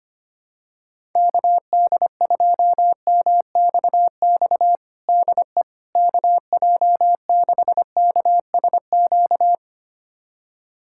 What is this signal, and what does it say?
Morse code, heard as a keyed tone, about 25 wpm, 700 Hz. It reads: KD2MXX DE KJ6KSQ
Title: USA Callsign DE Callsign 25wpm